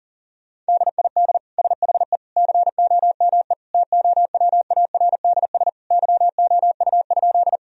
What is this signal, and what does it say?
Morse code, heard as a keyed tone, about 40 wpm, 700 Hz. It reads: DID SHE COME TOWARDS YOU?
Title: Sentences from Top 1000 Words - Review Part 2 - 40wpm